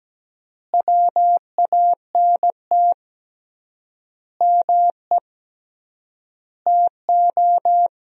Morse code, heard as a keyed tone, about 17 wpm, 700 Hz. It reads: WANT ME TO
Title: 3x Word-Spacing - Sentences from Top 200 Words - 17wpm